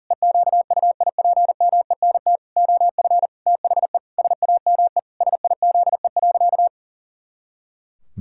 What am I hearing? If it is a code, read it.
Morse code, heard as a keyed tone, about 40 wpm, 700 Hz. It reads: EQUIPMENT OF THE SAME SIZE.
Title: Pangram Challenge v2 - 40wpm